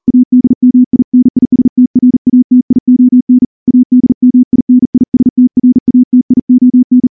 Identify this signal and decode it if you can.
Morse code, heard as a keyed tone, about 41 wpm, 250 Hz. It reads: ADMINISTRATION ADMINISTRATION
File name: morse / morse-training-40WPM